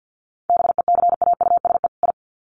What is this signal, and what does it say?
Morse code, heard as a keyed tone, about 50 wpm, 700 Hz. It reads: BECAUSE I
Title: Sets of 2 Words 50wpm